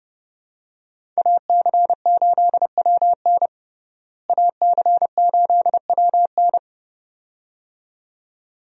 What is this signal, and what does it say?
Morse code, heard as a keyed tone, about 30 wpm, 700 Hz. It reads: AC8WN AC8WN
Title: USA Call Signs - 5 Characters - Repeated 2x - 30wpm